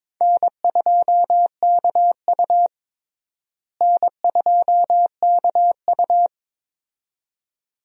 Morse code, heard as a keyed tone, about 22 wpm, 700 Hz. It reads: N2KU N2KU
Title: USA Call Signs - 4 Characters - Repeated 2x - 22wpm